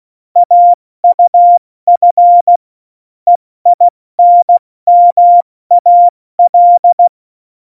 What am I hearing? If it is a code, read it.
Morse code, heard as a keyed tone, about 16 wpm, 700 Hz. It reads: AUF EINMAL
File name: Morsekurs / Witze / Witz 05. Februar 2019, 70 F80